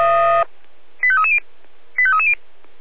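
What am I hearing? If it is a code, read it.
Five-tone selective call (ZVEI), heard as quick runs of five tones, tones of about 70 ms. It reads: 85330 85330
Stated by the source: ZVEI five-tone selective call 85269, received via SDR on channel 496.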